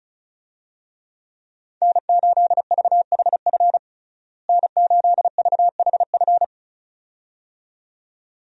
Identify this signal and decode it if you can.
Morse code, heard as a keyed tone, about 36 wpm, 700 Hz. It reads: N8VHF N8VHF
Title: USA Call Signs - 5 Characters - Repeated 2x - 35wpm